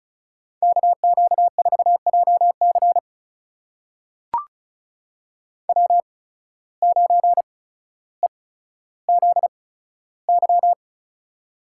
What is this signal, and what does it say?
Morse code, heard as a keyed tone, about 36 wpm, spaced as for 15 wpm, 700 Hz. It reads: KQ4JC W9EZY
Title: USA Call Signs - 5 Characters - 8x Character Spacing - 35wpm